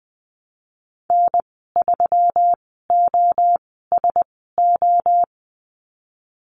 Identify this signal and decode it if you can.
Morse code, heard as a keyed tone, about 19 wpm, spaced as for 14 wpm, 700 Hz. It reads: N3OSO
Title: 154 Lesson 25g - Letter p - Call Signs 20wpm